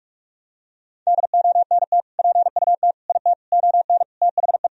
Morse code, heard as a keyed tone, about 46 wpm, 700 Hz. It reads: DONT PUT IT ON THE
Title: Sentences from Top 1000 Words - Review Part 2 - 45wpm